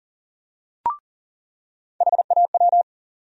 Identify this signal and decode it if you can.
Morse code, heard as a keyed tone, about 40 wpm, 700 Hz. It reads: HAW
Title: USA Call Sign Postfixes - 3 Letters 40wpm